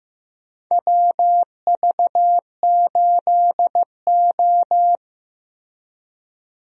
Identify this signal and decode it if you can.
Morse code, heard as a keyed tone, about 15 wpm, 700 Hz. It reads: WV8O
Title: USA Call Signs - 4 Characters - 15wpm